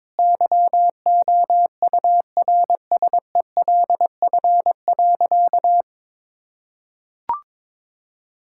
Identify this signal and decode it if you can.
Morse code, heard as a keyed tone, about 22 wpm, 700 Hz. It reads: YOURSELF.
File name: practice / 22wpm / Sentences with 8-Word Phrases - 22wpm